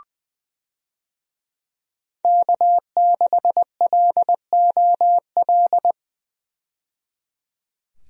Morse code, heard as a keyed tone, about 20 wpm, 700 Hz. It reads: K6LOL
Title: USA Call Signs - 5 Characters - 20wpm